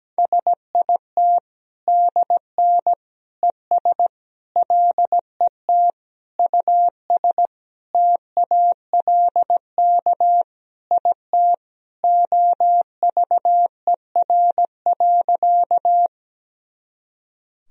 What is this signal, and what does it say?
Morse code, heard as a keyed tone, about 17 wpm, 700 Hz. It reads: SIT DN ES LET US TALK IT OVER.
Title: Sentences from Top 500 Words and Abbreviations 17wpm